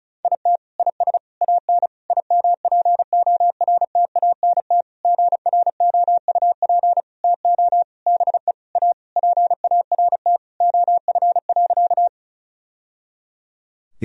Morse code, heard as a keyed tone, about 36 wpm, 700 Hz. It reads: IT IS AN IMPORTANT GROUP TO BE A PART OF.
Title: Sentences from Top 1000 Words - Review Part 1 - 35wpm